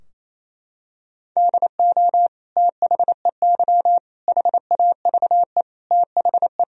Morse code, heard as a keyed tone, about 27 wpm, 700 Hz. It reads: DO THEY HAVE THE
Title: Sentences from Top 1000 Words - Review Part 3 - 28wpm